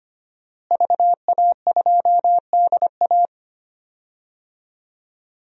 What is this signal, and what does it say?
Morse code, heard as a keyed tone, about 25 wpm, 700 Hz. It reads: VA2DA
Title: DXCC-100 Call Signs - 5 Characters - Repeated 3x - 25wpm